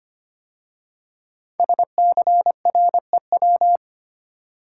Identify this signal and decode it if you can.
Morse code, heard as a keyed tone, about 25 wpm, 700 Hz. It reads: SCREW
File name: 5 Letter Words - First 3 Letters Given - 25wpm